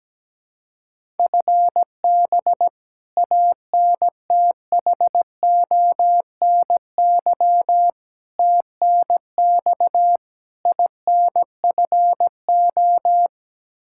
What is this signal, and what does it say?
Morse code, heard as a keyed tone, about 17 wpm, 700 Hz. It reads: FB ANTHONY TNX INFO
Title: K1USN SST QSO Practice 17wpm